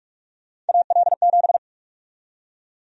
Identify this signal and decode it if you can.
Morse code, heard as a keyed tone, about 46 wpm, 700 Hz. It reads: AL7
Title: USA Call Sign Prefixes - 45wpm